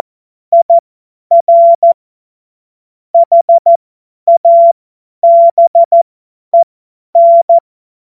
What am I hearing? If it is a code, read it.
Morse code, heard as a keyed tone, about 14 wpm, spaced as for 10 wpm, 680 Hz. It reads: IR HABEN
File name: Morsekurs / Witze / Witz 02. Dezember 2025 50 F70